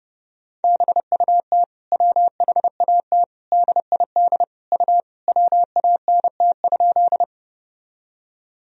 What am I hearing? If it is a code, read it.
Morse code, heard as a keyed tone, about 32 wpm, 700 Hz. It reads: BUT WHAT DID U WANT?
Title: Sentences from Top 200 Words and Abbreviations 30wpm